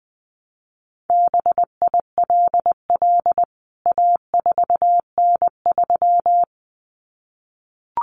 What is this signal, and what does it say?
Morse code, heard as a keyed tone, about 20 wpm, 700 Hz. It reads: BILL A4N3
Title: CWT Exchange 20wpm